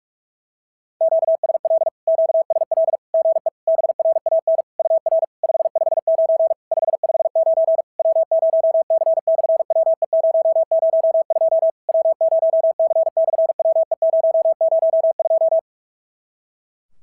Morse code, heard as a keyed tone, about 46 wpm, 650 Hz. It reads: QSL QSL GE BRAN UR 559 559 W0C/WE002 W0C/WE002
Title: SOTA QSO Practice 45wpm